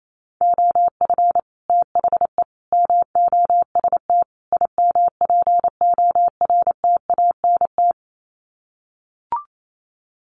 Morse code, heard as a keyed tone, about 28 wpm, 700 Hz. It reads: OF THE MOST IMPORTANT
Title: Sets of 4 Words 28wpm